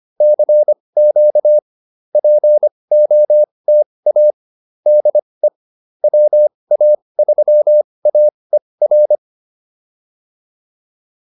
Morse code, heard as a keyed tone, about 25 wpm, spaced as for 20 wpm, 590 Hz. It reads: CQ POTA DE WA3AER